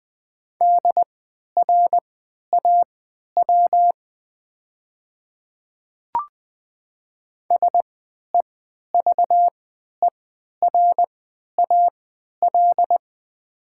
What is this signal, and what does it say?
Morse code, heard as a keyed tone, about 20 wpm, spaced as for 11 wpm, 700 Hz. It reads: DRAW SEVERAL
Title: 3x Character Spacing - Top 500 Words - 20wpm